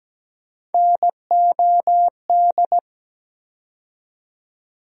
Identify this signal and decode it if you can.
Morse code, heard as a keyed tone, about 17 wpm, 700 Hz. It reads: NOD
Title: USA Call Sign Postfixes - 3 Letters 17wpm